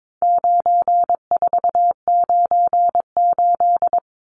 Morse code, heard as a keyed tone, about 22 wpm, 700 Hz. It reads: 9498
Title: Numbers - 4-Digit 22wpm